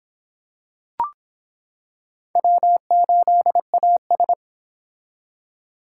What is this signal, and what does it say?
Morse code, heard as a keyed tone, about 26 wpm, 700 Hz. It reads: W8AS